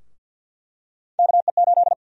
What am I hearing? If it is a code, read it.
Morse code, heard as a keyed tone, about 49 wpm, 700 Hz. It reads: KE8
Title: USA Call Sign Prefixes - 50wpm